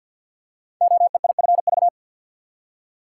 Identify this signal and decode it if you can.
Morse code, heard as a keyed tone, about 52 wpm, 700 Hz. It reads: OEIFV
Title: Code Groups - 5 Letters - 50wpm